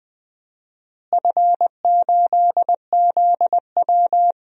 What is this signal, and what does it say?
Morse code, heard as a keyed tone, about 20 wpm, 700 Hz. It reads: F8ZW
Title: DXCC-100 Call Signs - 4 Characters - Repeated 3x - 20wpm